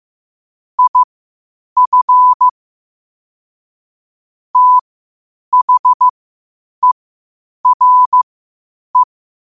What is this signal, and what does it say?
Morse code, heard as a keyed tone, about 15 wpm, spaced as for 8 wpm, 1000 Hz. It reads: IF THERE
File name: qotdfiles / 8wpm